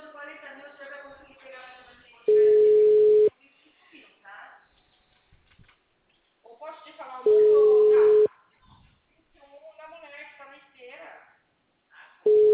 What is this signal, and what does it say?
Ringback tone (Europe, 425 Hz), with speech.
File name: out- -2014- - - .65808